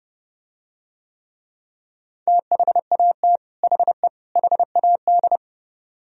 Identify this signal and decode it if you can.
Morse code, heard as a keyed tone, about 30 wpm, 700 Hz. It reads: THAT HE HAD